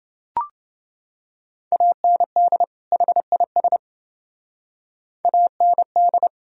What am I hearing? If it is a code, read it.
Morse code, heard as a keyed tone, about 28 wpm, 700 Hz. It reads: AND HIS AND